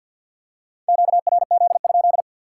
Morse code, heard as a keyed tone, about 53 wpm, 700 Hz. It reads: QRZ?
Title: Top 180 QSO Elements 50wpm